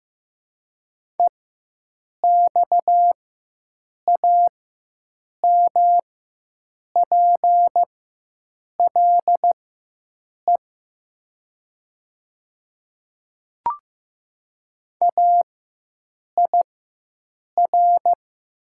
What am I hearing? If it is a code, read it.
Morse code, heard as a keyed tone, about 15 wpm, spaced as for 7 wpm, 700 Hz. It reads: EXAMPLE AIR
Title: 4x Character Spacing - Top 500 Words - 15wpm